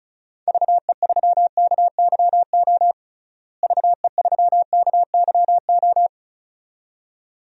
Morse code, heard as a keyed tone, about 34 wpm, 700 Hz. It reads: VE3KYO VE3KYO
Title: DXCC-100 Call Signs - 6 Characters - Repeated 2x - 35wpm